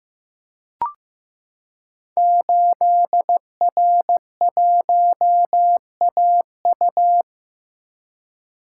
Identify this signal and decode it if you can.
Morse code, heard as a keyed tone, about 15 wpm, 700 Hz. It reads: 8R1AU
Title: Code Groups - 5 Letter:Numbers - 15wpm